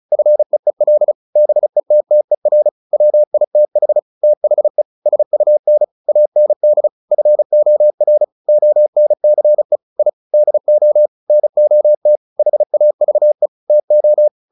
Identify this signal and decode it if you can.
Morse code, heard as a keyed tone, about 34 wpm, 600 Hz. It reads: FEEL BETTER WITH THE SUN AND FOR ONCE I DO NOT HAVE TO
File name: the old man and the sea / the old man and the sea 1 35wpm